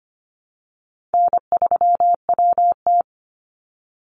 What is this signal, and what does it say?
Morse code, heard as a keyed tone, about 25 wpm, 700 Hz. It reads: N3WT
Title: Easy Call Signs - 25wpm